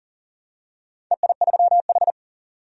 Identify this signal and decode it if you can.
Morse code, heard as a keyed tone, about 39 wpm, 700 Hz. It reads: EI3H
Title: DXCC-100 Call Signs - 4 Characters - 40wpm